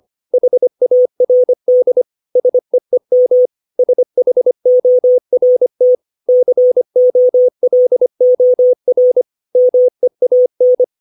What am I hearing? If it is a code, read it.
Morse code, heard as a keyed tone, about 25 wpm, 500 Hz. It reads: HARD SEEM SHORT COLOR MEAN
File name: 632 words / 632 Words 25WPM